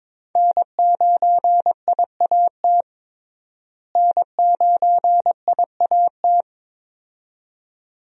Morse code, heard as a keyed tone, about 22 wpm, 700 Hz. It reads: N9IAT N9IAT
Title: USA Call Signs - 5 Characters - Repeated 2x - 22wpm